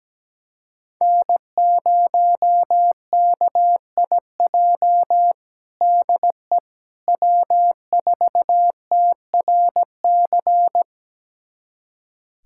Morse code, heard as a keyed tone, about 17 wpm, 700 Hz. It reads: N0KIJ DE W4TRC
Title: USA Callsign DE Callsign - 5 Characters - 17wpm